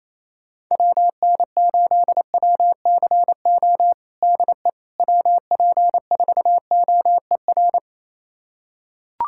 Morse code, heard as a keyed tone, about 28 wpm, 700 Hz. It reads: WN8WCO DE WP4OER
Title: USA Callsign DE Callsign - 6 Characters - 28wpm